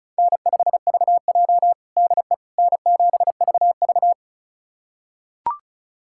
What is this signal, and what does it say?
Morse code, heard as a keyed tone, about 36 wpm, 700 Hz. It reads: N5VJ DE N7VV
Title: USA Callsign DE Callsign - 4 Characters - 35wpm